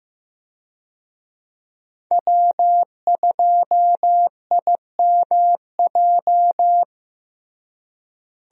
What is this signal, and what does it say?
Morse code, heard as a keyed tone, about 15 wpm, 700 Hz. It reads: W2IMJ
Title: USA Call Signs - 5 Characters - Repeated 3x - 15wpm